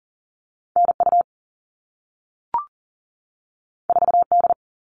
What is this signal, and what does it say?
Morse code, heard as a keyed tone, about 40 wpm, 700 Hz. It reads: NU 4D
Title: All Cut Numbers - 2-Digit 40wpm